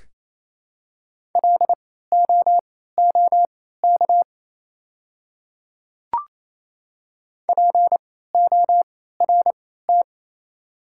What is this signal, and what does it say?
Morse code, heard as a keyed tone, about 28 wpm, spaced as for 16 wpm, 700 Hz. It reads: LOOK PORT
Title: 3x Character Spacing - Top 200 Words - 28wpm